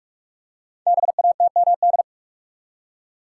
Morse code, heard as a keyed tone, about 44 wpm, 700 Hz. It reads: BATMD
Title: Code Groups - 5 Letters - 45wpm